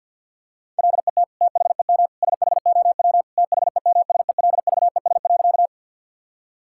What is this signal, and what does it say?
Morse code, heard as a keyed tone, about 52 wpm, 700 Hz. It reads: LET THEM SHOW THEMSELVES.
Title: Sentences from Top 1000 Words - Review Part 5 - 50wpm